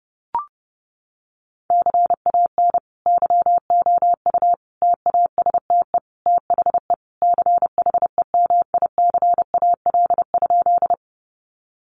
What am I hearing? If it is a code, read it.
Morse code, heard as a keyed tone, about 30 wpm, 700 Hz. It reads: CAN YOU TASTE THE CHEMICAL?
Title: Sentences from Top 1200 Words 30wpm